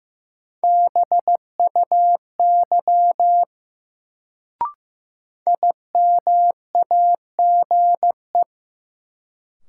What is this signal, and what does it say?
Morse code, heard as a keyed tone, about 15 wpm, 700 Hz. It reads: BUY IMAGE
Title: Sentences from Top 700 Words - New Word Drill - 15wpm